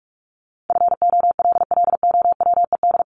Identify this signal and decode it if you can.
Morse code, heard as a keyed tone, about 44 wpm, 700 Hz. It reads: FOLLOWED
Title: ED Suffix - Top 100 Words with Base Words 45wpm